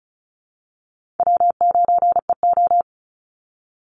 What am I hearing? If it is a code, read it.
Morse code, heard as a keyed tone, about 34 wpm, 700 Hz. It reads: W9EO
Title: Easy Call Signs Repeated 2x - 35wpm